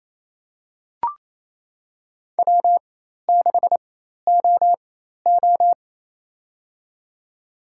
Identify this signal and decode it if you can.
Morse code, heard as a keyed tone, about 28 wpm, spaced as for 13 wpm, 700 Hz. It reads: W6OO
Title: USA Call Signs - 4 Characters - 4x Character Spacing - 28wpm